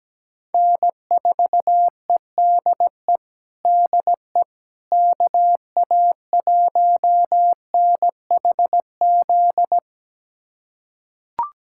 Morse code, heard as a keyed tone, about 17 wpm, 700 Hz. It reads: N4EDE DE KA1NHZ
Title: USA Callsign DE Callsign 17wpm